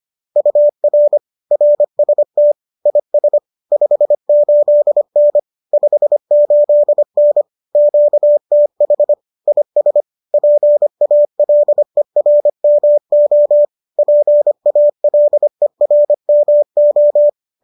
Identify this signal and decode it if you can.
Morse code, heard as a keyed tone, about 25 wpm, 590 Hz. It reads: UR RST IS 58N 58N QTH IS PALERMO PALERMO